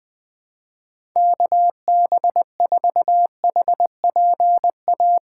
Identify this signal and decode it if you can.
Morse code, heard as a keyed tone, about 20 wpm, 700 Hz. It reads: KB4HPA
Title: USA Call Signs - 3x Character Spacing - 20wpm